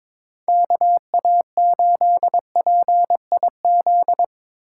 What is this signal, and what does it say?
Morse code, heard as a keyed tone, about 25 wpm, 700 Hz. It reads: KA8PIZ
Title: Speed Racing - USA 2x3 Call Signs 22wpm